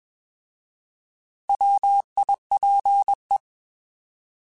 Morse code, heard as a keyed tone, about 21 wpm, 800 Hz. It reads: WIPE
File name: Alphabetical / Alphabetical mp3s / Ws @20